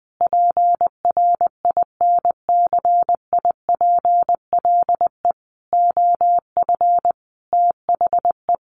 Morse code, heard as a keyed tone, about 20 wpm, 700 Hz. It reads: PRINCIPLE OF THE
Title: Sentences from Top 1500 Words 20wpm